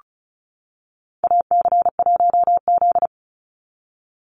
Morse code, heard as a keyed tone, about 34 wpm, 700 Hz. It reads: AC1Z
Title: NA SOTA Chaser Callsigns 35wpm